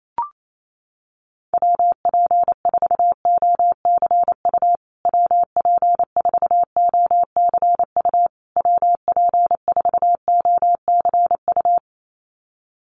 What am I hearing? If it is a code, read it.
Morse code, heard as a keyed tone, about 28 wpm, 700 Hz. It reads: WP4OCU WP4OCU WP4OCU
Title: USA Call Signs Repeated 3x 28wpm